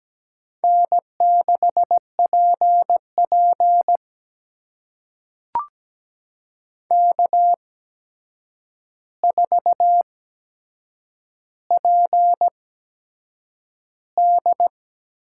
Morse code, heard as a keyed tone, about 17 wpm, spaced as for 7 wpm, 700 Hz. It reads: N6PP K4PD
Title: USA Call Signs - 4 Characters - 8x Character Spacing - 17wpm